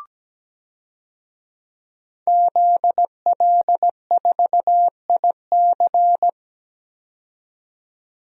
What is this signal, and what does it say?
Morse code, heard as a keyed tone, about 17 wpm, 700 Hz. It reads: ZL4IC
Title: DXCC-100 Call Signs - 5 Characters - 17wpm